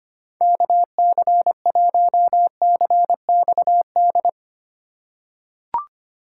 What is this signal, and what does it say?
Morse code, heard as a keyed tone, about 25 wpm, 700 Hz. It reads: KC1CXD
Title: USA Call Signs - 6 Characters - 3x Character Spacing - 25wpm